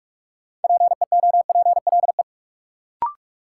Morse code, heard as a keyed tone, about 47 wpm, 700 Hz. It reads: PEOPLE
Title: Sets of 2 Words - New Word Drill - 45wpm